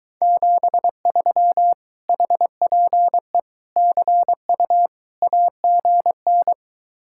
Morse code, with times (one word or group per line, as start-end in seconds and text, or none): 0.2-1.7 73
2.1-3.4 HPE
3.8-4.9 CU
5.2-6.5 AGN